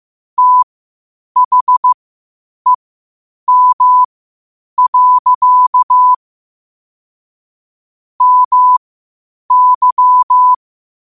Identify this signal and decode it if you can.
Morse code, heard as a keyed tone, about 15 wpm, spaced as for 8 wpm, 1000 Hz. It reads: THEM. MY
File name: qotdfiles / 8wpm